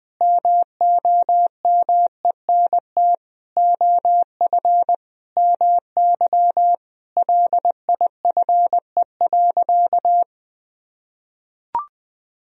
Morse code, heard as a keyed tone, about 21 wpm, 700 Hz. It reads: MOMENT OF MY LIFE.
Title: Sentences from Top 1000 Words - Review Part 2 - 20wpm